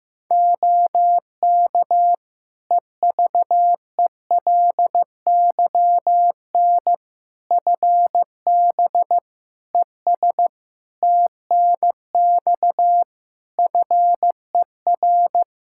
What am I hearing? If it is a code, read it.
Morse code, heard as a keyed tone, about 15 wpm, 700 Hz. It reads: OK EVELYN FB ES TNX FER
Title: Practice QSO - Exchange 2 - Info - 15wpm